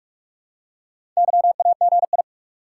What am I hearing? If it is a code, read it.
Morse code, heard as a keyed tone, about 47 wpm, 700 Hz. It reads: YAGI
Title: Top 180 QSO Elements 45wpm